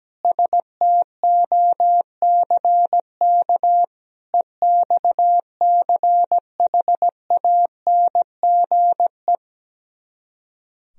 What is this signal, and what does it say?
Morse code, heard as a keyed tone, about 17 wpm, 700 Hz. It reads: STOCK EXCHANGE
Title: Speed-Racing - Sets of 5 Words 17wpm